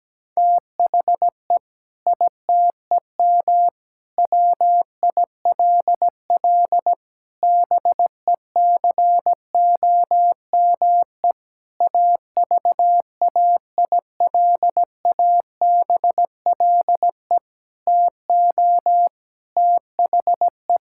Morse code, heard as a keyed tone, about 17 wpm, 700 Hz. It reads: THE ITEM WILL BECOME AVAILABLE TO THE
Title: Sentences from Top 1000 Words - Review Part 6 - 17wpm